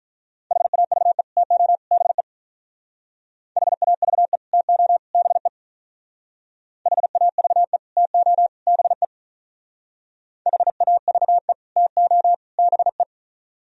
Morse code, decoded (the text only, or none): HAVE TO BE HAVE TO BE HAVE TO BE HAVE TO BE